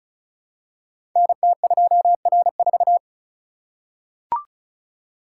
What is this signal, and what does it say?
Morse code, heard as a keyed tone, about 36 wpm, 700 Hz. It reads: NT2R4
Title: Code Groups - 5 Letter:Numbers - 35wpm